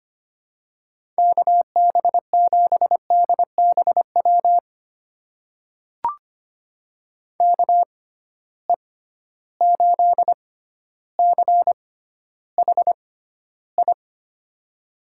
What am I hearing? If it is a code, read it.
Morse code, heard as a keyed tone, about 25 wpm, spaced as for 13 wpm, 700 Hz. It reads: KB7DBW KE8CHI
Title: USA Call Signs - 6 Characters - 6x Character Spacing - 25wpm